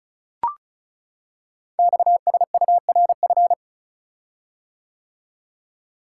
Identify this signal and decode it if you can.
Morse code, heard as a keyed tone, about 36 wpm, 690 Hz. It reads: XSURF
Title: Code Groups - 5 Letters - 35wpm